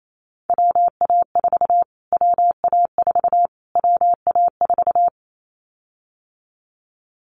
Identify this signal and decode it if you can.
Morse code, heard as a keyed tone, about 28 wpm, 700 Hz. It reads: WA4 WA4 WA4
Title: Callsign Prefixes for Top 25 DXCC - Repeated 3x - 28wpm